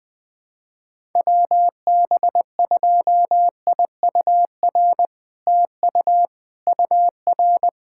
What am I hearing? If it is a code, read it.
Morse code, heard as a keyed tone, about 20 wpm, 700 Hz. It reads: WB2IUR TU UR
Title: SOTA QSO Practice 20wpm